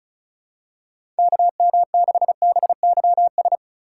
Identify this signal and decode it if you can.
Morse code, heard as a keyed tone, about 36 wpm, 700 Hz. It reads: KM6BYS
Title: USA Call Signs - 3x Character Spacing - 35wpm